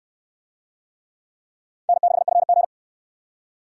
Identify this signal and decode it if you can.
Morse code, heard as a keyed tone, about 68 wpm, 700 Hz. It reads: N6FG